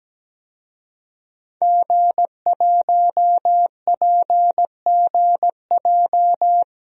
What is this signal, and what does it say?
Morse code, heard as a keyed tone, about 17 wpm, 700 Hz. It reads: G1PGJ
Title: DXCC-100 Call Signs - 5 Characters - 17wpm